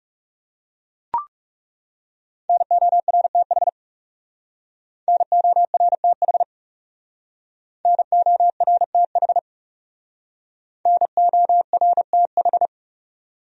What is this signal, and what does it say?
Morse code, heard as a keyed tone, about 36 wpm, 700 Hz. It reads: NORTH NORTH NORTH NORTH
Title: Speed-Racing Top 300 Words 30wpm